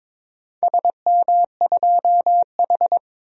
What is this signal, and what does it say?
Morse code, heard as a keyed tone, about 22 wpm, 700 Hz. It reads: SM2H
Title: DXCC-100 Call Signs - 4 Characters - Repeated 3x - 22wpm